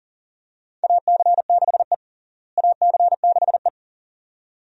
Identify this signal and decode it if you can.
Morse code, heard as a keyed tone, about 37 wpm, 700 Hz. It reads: AC6E AC6E
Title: USA Call Signs - 4 Characters - Repeated 2x - 40wpm